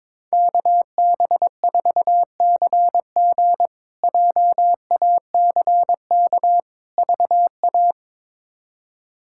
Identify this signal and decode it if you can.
Morse code, heard as a keyed tone, about 22 wpm, 700 Hz. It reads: KB4CG JACK VA
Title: K1USN SST QSO Practice - 22wpm